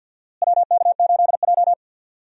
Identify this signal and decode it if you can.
Morse code, heard as a keyed tone, about 52 wpm, 700 Hz. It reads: WK8J
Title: USA Call Signs - 4 Characters - Repeated 2x - 50wpm